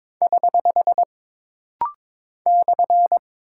Morse code, heard as a keyed tone, about 22 wpm, 700 Hz. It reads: <HH> /
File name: practice / 22wpm / Single Punctuation-Prosign - Rapid-Fire - 22wpm